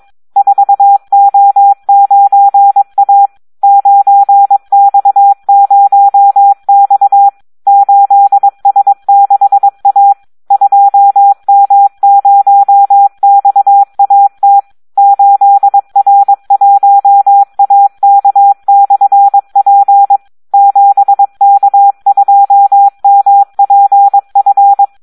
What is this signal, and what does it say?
Morse code, heard as a keyed tone, about 22 wpm, 800 Hz. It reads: 4O9A 9X0X 8S6A 2M0XAT 8R1AK/P 7K2MPF
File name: Audio practica CW / SD 1 22